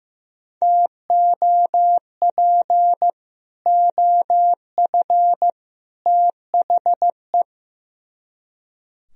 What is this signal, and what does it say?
Morse code, heard as a keyed tone, about 16 wpm, 700 Hz. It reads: TOP OF THE
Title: Speed-Racing - Sets of 3 Words 15wpm